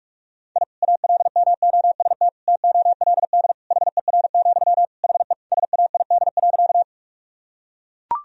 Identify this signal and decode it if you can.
Morse code, heard as a keyed tone, about 41 wpm, 700 Hz. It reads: I ALMOST TOLD HER, HE SAID.
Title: Sentences from Top 1000 Words - Review Part 4 - 45wpm